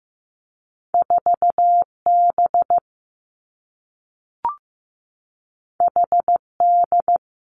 Morse code, heard as a keyed tone, about 15 wpm, 700 Hz. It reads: 4 B H D
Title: Two Challenging Letter-Number Combinations 15wpm